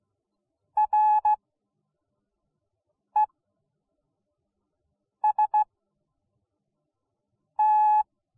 Morse code, heard as a keyed tone, about 16 wpm, spaced as for 4 wpm, 850 Hz. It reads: REST